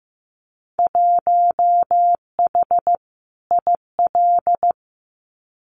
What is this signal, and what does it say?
Morse code, heard as a keyed tone, about 15 wpm, 700 Hz. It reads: 1H IL
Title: Winter Field Day Exchanges 15wpm